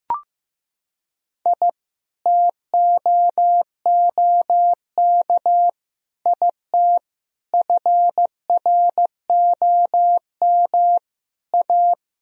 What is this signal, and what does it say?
Morse code, heard as a keyed tone, about 15 wpm, 700 Hz. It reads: I TOOK IT FROM A
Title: Sentences from Top 1000 Words - Review Part 1 - 15wpm